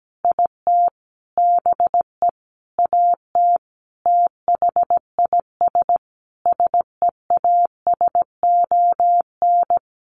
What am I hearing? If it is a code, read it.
Morse code, heard as a keyed tone, about 17 wpm, 700 Hz. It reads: IT BE AT THIS SEASON
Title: Sentences from Top 1400 Words 17wpm